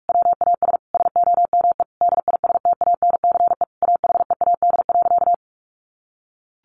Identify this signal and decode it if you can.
Morse code, heard as a keyed tone, about 44 wpm, 700 Hz. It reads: WAS SOME DISTANCE AHEAD.
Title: Sentences from Top 1400 Words 45wpm